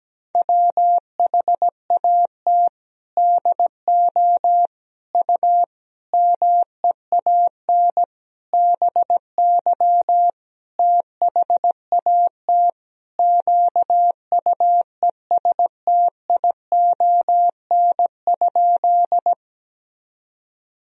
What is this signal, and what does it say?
Morse code, heard as a keyed tone, about 17 wpm, 700 Hz. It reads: WHAT DO U MEAN BY THAT QUESTION?
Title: Sentences from Top 400 Words and Abbreviations 17wpm